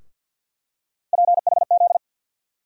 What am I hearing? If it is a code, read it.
Morse code, heard as a keyed tone, about 47 wpm, 690 Hz. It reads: PHZ
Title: USA Call Sign Postfixes - 3 Letters 50wpm